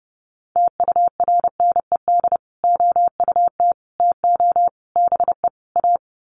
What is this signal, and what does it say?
Morse code, heard as a keyed tone, about 28 wpm, 700 Hz. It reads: TURNED OUT TO BE A
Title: Sets of 5 Words 30wpm